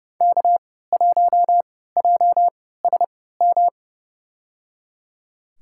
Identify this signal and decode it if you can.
Morse code, heard as a keyed tone, about 30 wpm, spaced as for 17 wpm, 700 Hz. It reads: K1JSM
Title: USA Call Signs - 3x Character Spacing - 30wpm